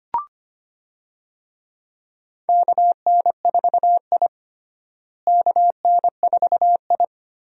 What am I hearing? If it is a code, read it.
Morse code, heard as a keyed tone, about 25 wpm, 700 Hz. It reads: KN4I KN4I
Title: USA Call Signs - 4 Characters - Repeated 3x - 25wpm